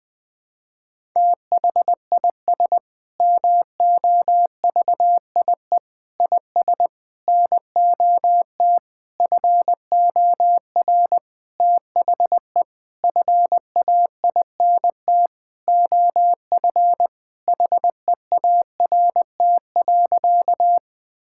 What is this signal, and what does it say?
Morse code, heard as a keyed tone, about 20 wpm, 700 Hz. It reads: THIS MOVIE IS NOT FOR THE FAINT OF HEART.
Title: Sentences with 6-Word Phrases - 20wpm